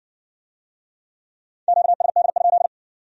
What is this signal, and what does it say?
Morse code, heard as a keyed tone, about 57 wpm, 700 Hz. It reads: QSD?